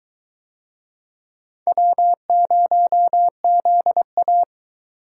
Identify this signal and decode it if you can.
Morse code, heard as a keyed tone, about 22 wpm, 700 Hz. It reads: W0ZA